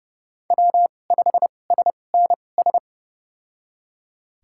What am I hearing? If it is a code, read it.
Morse code, heard as a keyed tone, about 30 wpm, spaced as for 22 wpm, 700 Hz. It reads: W5SNS
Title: USA Call Signs - 5 Characters - 2x Character Spacing - 30wpm